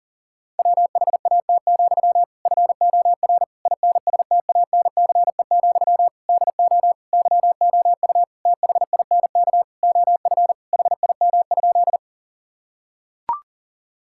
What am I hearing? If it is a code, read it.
Morse code, heard as a keyed tone, about 37 wpm, 700 Hz. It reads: WHAT, FOR INSTANCE, DO YOU THINK OF HIM?
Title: Sentences from Top 1000 Words - Review Part 5 - 40wpm